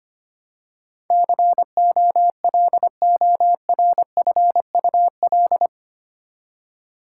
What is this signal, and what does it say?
Morse code, heard as a keyed tone, about 25 wpm, 700 Hz. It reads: COLORFUL
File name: 8 Letter Words - First 3 Letters Given - 25wpm